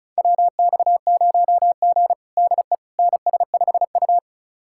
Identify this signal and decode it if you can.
Morse code, heard as a keyed tone, about 36 wpm, 700 Hz. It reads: WX0G DE NS5U
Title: USA Callsign DE Callsign - 4 Characters - 35wpm